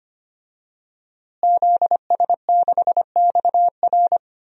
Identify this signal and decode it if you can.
Morse code, heard as a keyed tone, about 25 wpm, 700 Hz. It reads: ZS6XR
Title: DXCC-100 Call Signs - 5 Characters - Repeated 3x - 25wpm